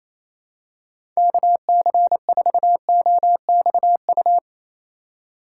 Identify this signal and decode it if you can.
Morse code, heard as a keyed tone, about 28 wpm, 700 Hz. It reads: KC4OXU
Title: USA Call Signs - 6 Characters - 2x Character Spacing - 28wpm